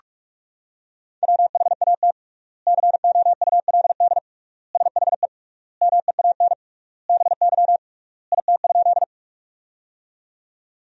Morse code, heard as a keyed tone, about 41 wpm, 700 Hz. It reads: WHAT COULD SHE MEAN BY IT?
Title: 3x Word-Spacing - Sentences from Top 200 Words - 45wpm